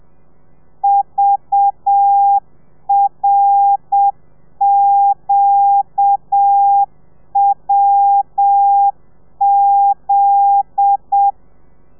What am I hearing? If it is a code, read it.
Morse code, heard as a keyed tone, about 7 wpm, 800 Hz. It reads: VRQWZ